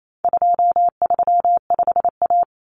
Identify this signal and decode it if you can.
Morse code, heard as a keyed tone, about 28 wpm, 700 Hz. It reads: 235A
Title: Common Cut Numbers - 4-Digit 28wpm